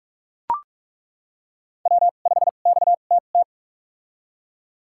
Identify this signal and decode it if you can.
Morse code, heard as a keyed tone, about 41 wpm, spaced as for 31 wpm, 700 Hz. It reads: W5XTT
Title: USA Call Signs - 5 Characters - 2x Character Spacing - 45wpm